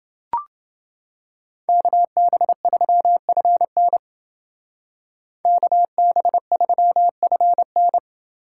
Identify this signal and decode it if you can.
Morse code, heard as a keyed tone, about 28 wpm, 700 Hz. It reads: KB3FN KB3FN